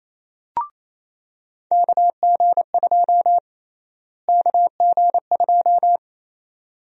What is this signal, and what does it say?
Morse code, heard as a keyed tone, about 28 wpm, 700 Hz. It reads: KG2 KG2
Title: USA Call Sign Prefixes - Repeated 2x - 28wpm